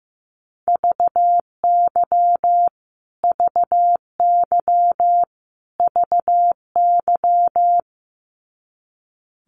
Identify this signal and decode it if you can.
Morse code, heard as a keyed tone, about 15 wpm, 700 Hz. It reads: VY VY VY
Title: Abbreviations Repeated 3x - 15wpm